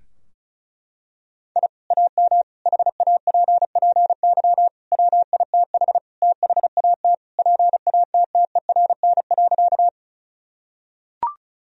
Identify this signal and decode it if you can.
Morse code, heard as a keyed tone, about 34 wpm, 700 Hz. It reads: I AM HAPPY WITH THAT PATTERN.
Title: Sentences from Top 1000 Words - Review Part 2 - 35wpm